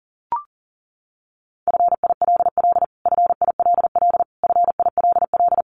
Morse code, heard as a keyed tone, about 37 wpm, 700 Hz. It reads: FILL FILL FILL
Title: Top 500 Words Repeated 40wpm